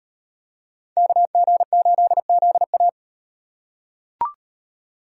Morse code, heard as a keyed tone, about 38 wpm, 700 Hz. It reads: KG8ZA